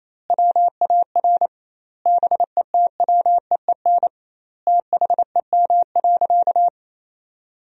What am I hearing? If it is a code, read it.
Morse code, heard as a keyed tone, about 28 wpm, 700 Hz. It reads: WAR BETWEEN THEM.
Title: 2x Word-Spacing - Sentences from Top 400 Words - 28wpm